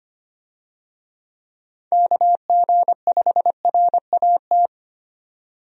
Morse code, heard as a keyed tone, about 25 wpm, 700 Hz. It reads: KG5RAT
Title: USA Call Signs - 6 Characters - 25wpm